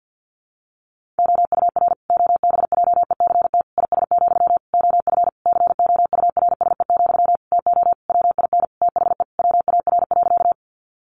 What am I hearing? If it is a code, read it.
Morse code, heard as a keyed tone, about 50 wpm, 700 Hz. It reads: OUR OBJECT IS, OF COURSE, TO WIN THE WAR.
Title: Sentences from Top 1500 Words 50wpm